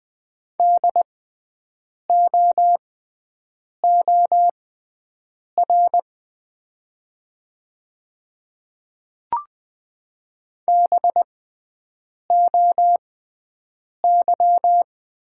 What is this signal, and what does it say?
Morse code, heard as a keyed tone, about 20 wpm, spaced as for 7 wpm, 700 Hz. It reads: DOOR BOY
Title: 6x Character Spacing - Top 500 Words - 20wpm